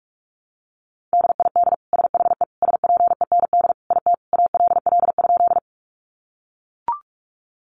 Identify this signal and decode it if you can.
Morse code, heard as a keyed tone, about 44 wpm, 700 Hz. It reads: DID SHE SPEND IT ALL?
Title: Sentences from Top 800 Words 45wpm